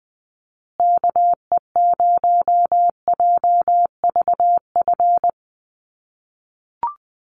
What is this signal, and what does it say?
Morse code, heard as a keyed tone, about 20 wpm, 700 Hz. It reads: KE0JVF
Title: USA Call Signs Repeated 2x 20wpm